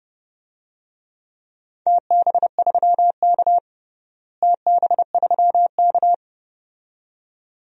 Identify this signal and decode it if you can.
Morse code, heard as a keyed tone, about 30 wpm, 700 Hz. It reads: TB3K TB3K
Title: DXCC-100 Call Signs - 4 Characters - Repeated 2x - 30wpm